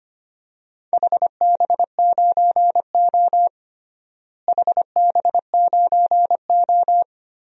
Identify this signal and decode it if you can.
Morse code, heard as a keyed tone, about 25 wpm, 700 Hz. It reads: HB9O HB9O
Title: DXCC-100 Call Signs - 4 Characters - Repeated 3x - 25wpm